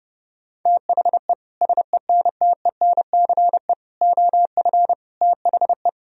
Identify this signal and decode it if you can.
Morse code, heard as a keyed tone, about 30 wpm, 700 Hz. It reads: THE SENTENCE OF THE
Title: Sentences from Top 1000 Words - Review Part 1 - 30wpm